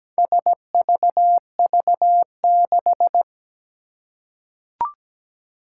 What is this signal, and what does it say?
Morse code, heard as a keyed tone, about 17 wpm, 700 Hz. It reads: SVV6
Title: Four Challenging Letter-Number Combinations 17wpm